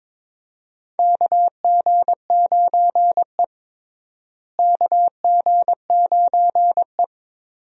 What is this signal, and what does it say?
Morse code, heard as a keyed tone, about 22 wpm, 700 Hz. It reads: KG9E KG9E
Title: USA Call Signs - 4 Characters - Repeated 2x - 22wpm